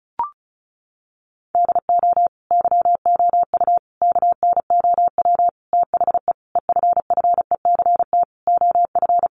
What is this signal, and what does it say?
Morse code, heard as a keyed tone, about 34 wpm, 700 Hz. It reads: DO YOU KNOW THE EFFECT OF
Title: Sentences from Top 1400 Words 35wpm